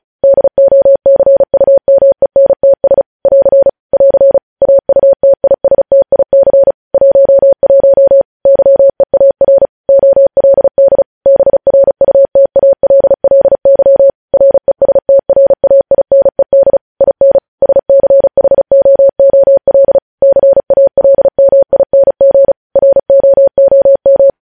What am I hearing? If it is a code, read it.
Morse code, heard as a keyed tone, about 36 wpm, 550 Hz. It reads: DOCUMENTS <AR> <AR> AUTISTIC 11 YEAR OLD BRUTALLY RESTRAINED IN SCHOOL CALMING ROOM
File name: podcast / News headlines-1x-35x35